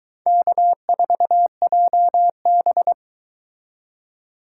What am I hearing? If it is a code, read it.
Morse code, heard as a keyed tone, about 22 wpm, 700 Hz. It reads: K4JB